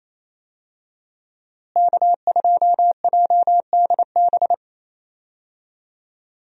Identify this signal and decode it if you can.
Morse code, heard as a keyed tone, about 28 wpm, 700 Hz. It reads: K2JDB
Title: USA Call Signs - 5 Characters - Repeated 3x - 28wpm